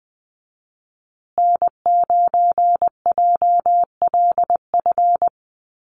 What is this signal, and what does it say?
Morse code, heard as a keyed tone, about 20 wpm, 700 Hz. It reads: N9JLF
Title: USA Call Signs Repeated 2x 20wpm